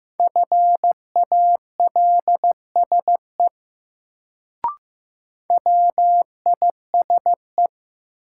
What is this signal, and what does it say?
Morse code, heard as a keyed tone, about 15 wpm, 700 Hz. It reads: FALSE WISE
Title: Sentences from Top 1600 Words - New Word Drill Alt - 15wpm